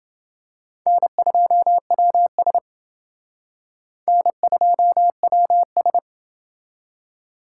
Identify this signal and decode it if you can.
Morse code, heard as a keyed tone, about 30 wpm, 700 Hz. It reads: N2WS N2WS